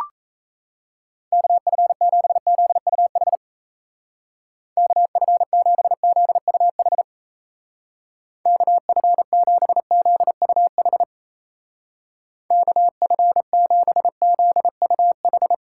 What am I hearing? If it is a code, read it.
Morse code, heard as a keyed tone, about 35 wpm, 700 Hz. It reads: KF7ZUH KF7ZUH KF7ZUH KF7ZUH
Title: Speed Racing - USA 2x3 Call Signs 28wpm